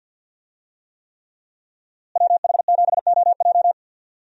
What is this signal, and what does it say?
Morse code, heard as a keyed tone, about 52 wpm, 700 Hz. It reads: WH7OJ
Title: USA Call Signs - 5 Characters - Repeated 3x - 50wpm